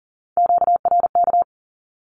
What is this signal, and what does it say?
Morse code, heard as a keyed tone, about 40 wpm, 700 Hz. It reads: QRK
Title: Q-Codes Repeated 3x 40wpm